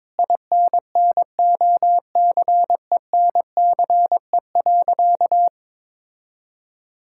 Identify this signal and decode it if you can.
Morse code, heard as a keyed tone, about 22 wpm, 700 Hz. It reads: INNOCENCE.
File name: Sentences with 7-Word Phrases - 22wpm